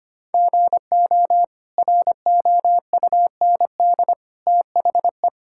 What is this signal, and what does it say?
Morse code, heard as a keyed tone, about 25 wpm, 700 Hz. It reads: GO ROUND THE
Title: Sentences from Top 1600 Words 25wpm